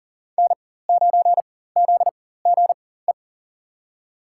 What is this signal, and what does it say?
Morse code, heard as a keyed tone, about 40 wpm, spaced as for 19 wpm, 700 Hz. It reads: N9ZGE
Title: USA Call Signs - 5 Characters - 4x Character Spacing - 40wpm